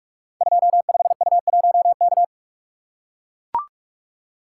Morse code, heard as a keyed tone, about 44 wpm, 700 Hz. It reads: 25U1K
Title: Code Groups - 5 Letter:Numbers - 45wpm